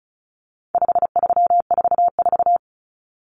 Morse code, heard as a keyed tone, about 34 wpm, 700 Hz. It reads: 5344
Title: Numbers - 4-Digit 35wpm